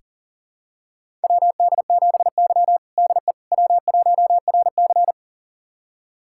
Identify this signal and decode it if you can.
Morse code, heard as a keyed tone, about 40 wpm, 700 Hz. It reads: WD7Y DE W1RC
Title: USA Callsign DE Callsign - 4 Characters - 40wpm